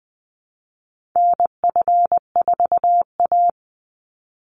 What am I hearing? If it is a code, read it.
Morse code, heard as a keyed tone, about 20 wpm, 700 Hz. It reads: NF4A
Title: Call Signs - 20wpm